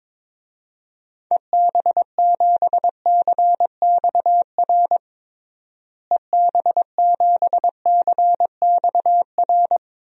Morse code, heard as a keyed tone, about 22 wpm, 700 Hz. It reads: EB7CXR EB7CXR
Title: DXCC-100 Call Signs - 6 Characters - Repeated 3x - 22wpm